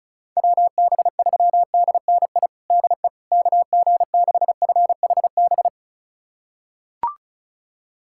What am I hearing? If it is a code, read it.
Morse code, heard as a keyed tone, about 34 wpm, 700 Hz. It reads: WB3DNI DE KG6FHB
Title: USA Callsign DE Callsign - 6 Characters - 35wpm